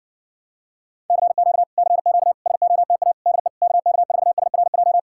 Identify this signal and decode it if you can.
Morse code, heard as a keyed tone, about 58 wpm, 700 Hz. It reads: CQ CQ SOTA DE KK4SRJ